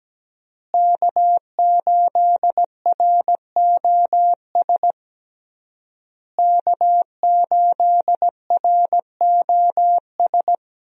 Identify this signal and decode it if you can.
Morse code, heard as a keyed tone, about 17 wpm, 700 Hz. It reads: K8ROS K8ROS
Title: USA Call Signs - 5 Characters - Repeated 3x - 17wpm